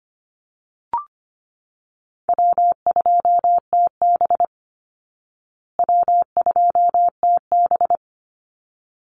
Morse code, heard as a keyed tone, about 25 wpm, 700 Hz. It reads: W2TB W2TB
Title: Easy Call Signs Repeated 2x - 25wpm